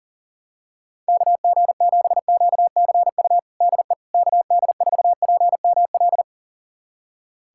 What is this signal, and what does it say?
Morse code, heard as a keyed tone, about 40 wpm, 700 Hz. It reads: KG7QCU DE KD4PML
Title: USA Callsign DE Callsign - 6 Characters - 40wpm